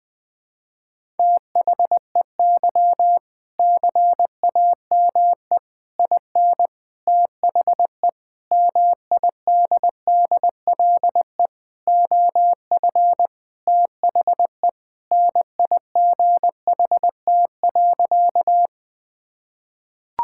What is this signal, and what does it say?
Morse code, heard as a keyed tone, about 20 wpm, 700 Hz. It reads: THEY CAME IN THE MIDDLE OF THE NIGHT.
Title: Sentences from Top 1000 Words - Review Part 6 - 20wpm